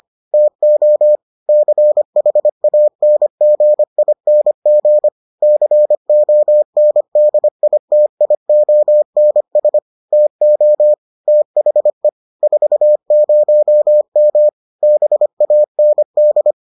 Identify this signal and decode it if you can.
Morse code, heard as a keyed tone, about 25 wpm, 600 Hz. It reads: TO CHANGING CONDITIONS TO THE 40M BAND